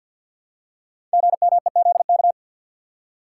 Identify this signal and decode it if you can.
Morse code, heard as a keyed tone, about 52 wpm, 700 Hz. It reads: GMEZK
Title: Code Groups - 5 Letter:Numbers - 50wpm